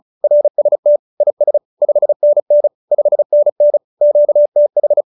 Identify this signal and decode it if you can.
Morse code, heard as a keyed tone, about 34 wpm, 590 Hz. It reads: RST IS 5NN 5NN QTH
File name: QSO302 35